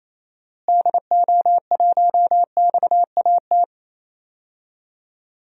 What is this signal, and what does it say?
Morse code, heard as a keyed tone, about 28 wpm, 700 Hz. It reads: DO1XAT
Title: DXCC-100 Call Signs - 6 Characters - Repeated 2x - 28wpm